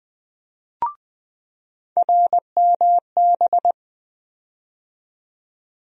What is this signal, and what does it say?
Morse code, heard as a keyed tone, about 20 wpm, 700 Hz. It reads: RMB
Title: USA Call Sign Postfixes - 3 Letters 20wpm